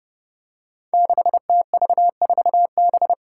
Morse code, heard as a keyed tone, about 30 wpm, 700 Hz. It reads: 6TV4B
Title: Five Challenging Letter-Number Combinations 30wpm